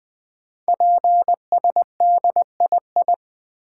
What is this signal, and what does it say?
Morse code, heard as a keyed tone, about 20 wpm, 700 Hz. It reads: PSDII
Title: Code Groups - 5 Letter:Numbers - 20wpm